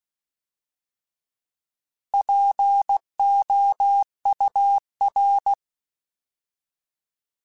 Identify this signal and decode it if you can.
Morse code, heard as a keyed tone, about 16 wpm, 800 Hz. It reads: POUR
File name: Alphabetical / Alphabetical mp3s / Ps @15